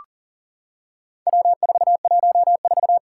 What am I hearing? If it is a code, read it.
Morse code, heard as a keyed tone, about 40 wpm, 700 Hz. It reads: W414
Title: Four Challenging Letter-Number Combinations 40wpm